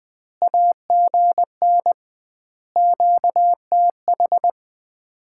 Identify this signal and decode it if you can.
Morse code, heard as a keyed tone, about 20 wpm, 700 Hz. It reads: AGN QTH
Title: Common QSO Phrases - 2x Word-Spacing - 20wpm